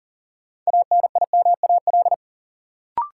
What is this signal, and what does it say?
Morse code, heard as a keyed tone, about 39 wpm, 700 Hz. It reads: ANIMAL
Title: Sentences from Top 1000 Words - Review Part 1 - New Word Drill - 40wpm